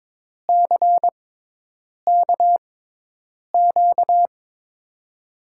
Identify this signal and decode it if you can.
Morse code, heard as a keyed tone, about 22 wpm, spaced as for 8 wpm, 700 Hz. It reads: CKQ
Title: USA Call Signs - 6 Characters - 6x Character Spacing - 22wpm